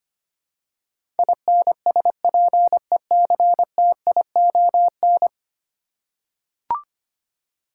Morse code, heard as a keyed tone, about 25 wpm, 700 Hz. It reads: INSPECTION
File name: Ten Letter Words - 25wpm